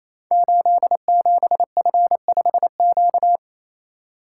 Morse code, heard as a keyed tone, about 27 wpm, 700 Hz. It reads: 87F5Q
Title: Code Groups - 5 Letter:Numbers - 28wpm